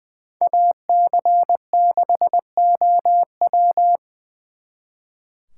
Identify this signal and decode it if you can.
Morse code, heard as a keyed tone, about 25 wpm, 700 Hz. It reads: AC6OW
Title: Speed Racing - USA Call Signs 20wpm